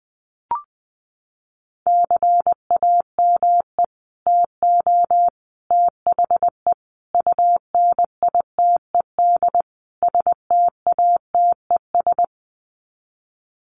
Morse code, heard as a keyed tone, about 20 wpm, 700 Hz. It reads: CAME TO THE UNITED STATES
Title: Sets of 5 Words Encore2 20wpm